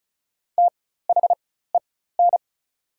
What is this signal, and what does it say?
Morse code, heard as a keyed tone, about 36 wpm, spaced as for 16 wpm, 700 Hz. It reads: THEN
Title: 4x Character Spacing - Top 100 Words - 35wpm